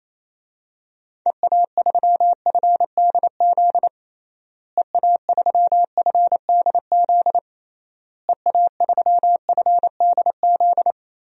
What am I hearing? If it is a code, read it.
Morse code, heard as a keyed tone, about 28 wpm, 700 Hz. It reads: EA3FDZ EA3FDZ EA3FDZ
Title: DXCC-100 Call Signs - 6 Characters - Repeated 3x - 28wpm